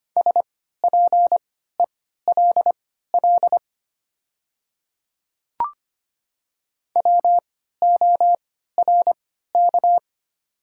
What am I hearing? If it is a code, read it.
Morse code, heard as a keyed tone, about 25 wpm, spaced as for 14 wpm, 700 Hz. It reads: SPELL WORK
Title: 3x Character Spacing - Top 200 Words - 25wpm